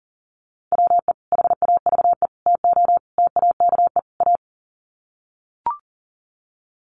Morse code, heard as a keyed tone, about 37 wpm, 700 Hz. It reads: WE HAVE TO TAKE A
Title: Sets of 5 Words Encore 40wpm